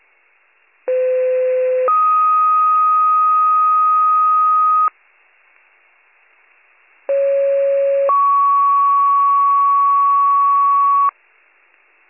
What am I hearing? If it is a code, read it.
Two-tone page, heard as a one-second tone, then a three-second tone, about 510 then 1250 Hz, about 550 then 1120 Hz. It is heard from 0.9 to 11.1 s.